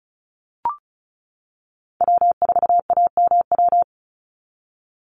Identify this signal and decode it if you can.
Morse code, heard as a keyed tone, about 34 wpm, 700 Hz. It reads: W4AMW
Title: NA SOTA Chaser Callsigns 35wpm